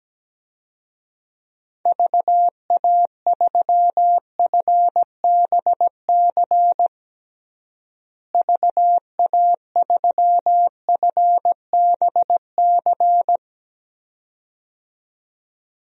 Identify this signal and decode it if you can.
Morse code, heard as a keyed tone, about 17 wpm, 700 Hz. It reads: VA3FBC VA3FBC
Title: DXCC-100 Call Signs - 6 Characters - Repeated 2x - 17wpm